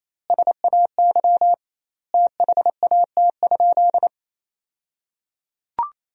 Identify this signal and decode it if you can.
Morse code, heard as a keyed tone, about 28 wpm, 700 Hz. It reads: SAY THAT?
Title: 2x Word-Spacing - Sentences from Top 200 Words - 28wpm